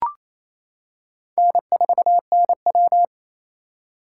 Morse code, heard as a keyed tone, about 28 wpm, 700 Hz. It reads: N4NW
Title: K1USN SST QSO Practice - 28wpm